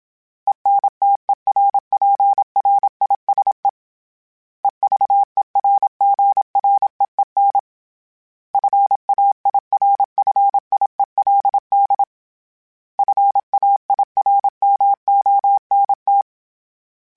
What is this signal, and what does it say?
Morse code, heard as a keyed tone, about 27 wpm, 800 Hz. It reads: ENTERPRISE EVERGREEN FAIRFIELD FAIRMONT
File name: fromSMs / FromSMs mp3s / Cities@25